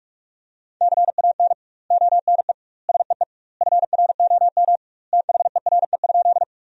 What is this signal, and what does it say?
Morse code, heard as a keyed tone, about 41 wpm, 700 Hz. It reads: CAN ONE SEE FROM THERE?
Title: 2x Word-Spacing - Sentences from Top 100 Words - 45wpm